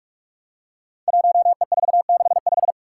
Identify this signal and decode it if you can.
Morse code, heard as a keyed tone, about 44 wpm, 700 Hz. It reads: 1E465
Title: Code Groups - 5 Letter:Numbers - 45wpm